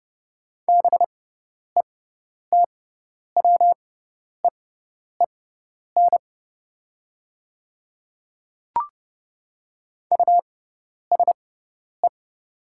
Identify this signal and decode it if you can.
Morse code, heard as a keyed tone, about 30 wpm, spaced as for 10 wpm, 700 Hz. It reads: BETWEEN USE
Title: 6x Character Spacing - Top 300 Words - 30wpm